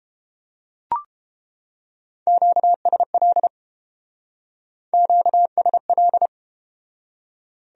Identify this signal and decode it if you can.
Morse code, heard as a keyed tone, about 30 wpm, 700 Hz. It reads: QSL QSL